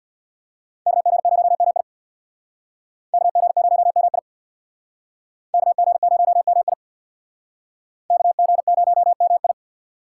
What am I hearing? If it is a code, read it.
Morse code, heard as a keyed tone, about 64 wpm, 700 Hz. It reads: KG0MI KG0MI KG0MI KG0MI